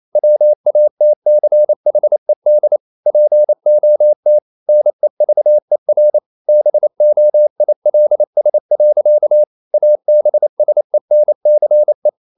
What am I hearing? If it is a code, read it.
Morse code, heard as a keyed tone, about 28 wpm, 600 Hz. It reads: WATCHED POT NEVER BOILS. ABSENCE
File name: CW / Practice / EnglishSayings28